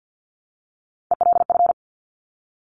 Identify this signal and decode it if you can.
Morse code, heard as a keyed tone, about 50 wpm, 700 Hz. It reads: ELF
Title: Top Three Letter Combinations in Words 50wpm